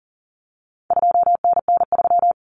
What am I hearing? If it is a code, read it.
Morse code, heard as a keyed tone, about 37 wpm, 700 Hz. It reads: 2NN3
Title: Common Cut Numbers - 4-Digit 40wpm